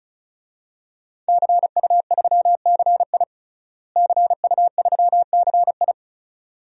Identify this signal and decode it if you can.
Morse code, heard as a keyed tone, about 34 wpm, 700 Hz. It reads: CU3CI CU3CI
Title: DXCC-100 Call Signs - 5 Characters - Repeated 2x - 35wpm